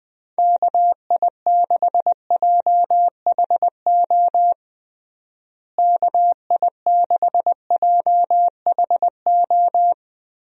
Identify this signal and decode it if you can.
Morse code, heard as a keyed tone, about 20 wpm, 700 Hz. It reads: KI6JHO KI6JHO
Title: USA Call Signs - 6 Characters - Repeated 2x - 20wpm